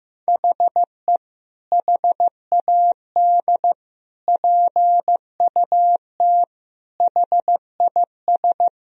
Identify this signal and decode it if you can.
Morse code, heard as a keyed tone, about 15 wpm, 700 Hz. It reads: HE HAD PUT HIS
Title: Sentences from Top 1000 Words - Review Part 3 - 15wpm